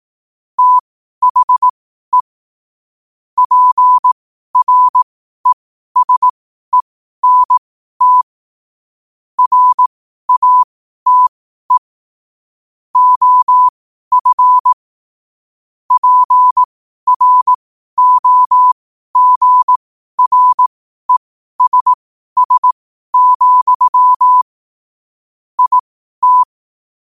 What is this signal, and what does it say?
Morse code, heard as a keyed tone, about 18 wpm, spaced as for 12 wpm, 1000 Hz. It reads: THE PRESENT RATE OF PROGRESS, IT
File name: qotdfiles / 12wpm